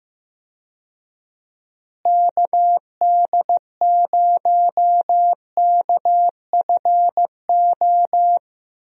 Morse code, heard as a keyed tone, about 15 wpm, 700 Hz. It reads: KD0KFO
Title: USA Call Signs - 6 Characters - Repeated 3x - 15wpm